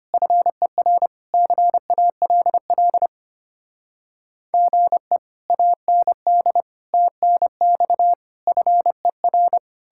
Morse code, heard as a keyed tone, about 33 wpm, 700 Hz. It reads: FER CALL GE AND TNX FER